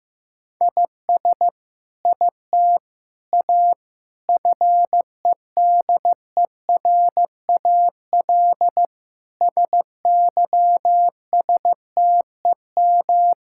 Morse code, heard as a keyed tone, about 15 wpm, 700 Hz. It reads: IS IT A FEDERAL SYSTEM
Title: Sentences from Top 1000 Words - Review Part 5 - 15wpm